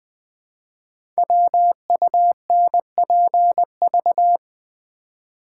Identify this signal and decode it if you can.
Morse code, heard as a keyed tone, about 20 wpm, 700 Hz. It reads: WUNPV
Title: Code Groups - 5 Letters - 20wpm